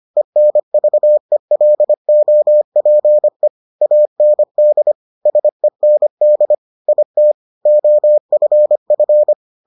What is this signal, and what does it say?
Morse code, heard as a keyed tone, about 25 wpm, 600 Hz. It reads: ENVELOPE AND SEND IT OFF